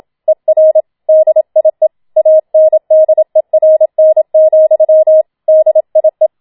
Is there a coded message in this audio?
Yes: Morse code from 0.3 to 6.3 s, heard as a keyed tone, about 27 wpm, 630 Hz. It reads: ER DIE ANDERN, DIE